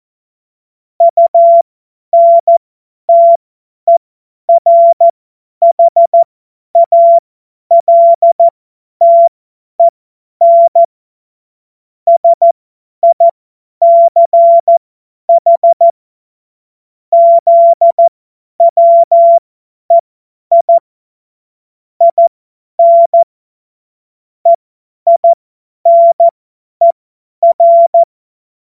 Morse code, heard as a keyed tone, about 14 wpm, spaced as for 10 wpm, 680 Hz. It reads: UNTERHALTEN SICH ZWEI IN EINER
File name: Morsekurs / Witze / Witz 20. Juli 2021, 50 F70